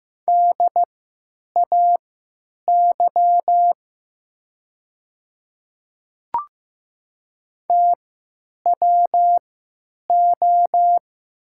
Morse code, heard as a keyed tone, about 15 wpm, spaced as for 9 wpm, 700 Hz. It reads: DAY TWO
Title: 3x Character Spacing - Top 200 Words - 15wpm